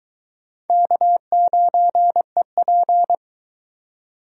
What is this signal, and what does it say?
Morse code, heard as a keyed tone, about 23 wpm, 700 Hz. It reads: K9EP